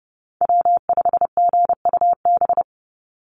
Morse code, heard as a keyed tone, about 30 wpm, 700 Hz. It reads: W5GUB
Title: USA Call Signs 30wpm